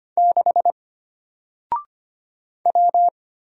Morse code, heard as a keyed tone, about 25 wpm, 700 Hz. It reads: 6 W
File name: Single-Letter-Number-25wpm